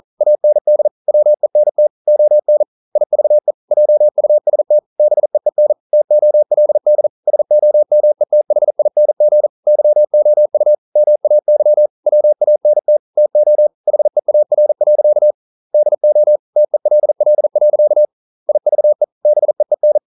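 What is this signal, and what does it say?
Morse code, heard as a keyed tone, about 42 wpm, 600 Hz. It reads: AND WENT ON IVE JUST BEEN TOLD SOMETHING YOU MAY WANT TO HEAR. DO TELL. IVE BEEN
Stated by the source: Morse code practice at 41 wpm, from CW med Gnister.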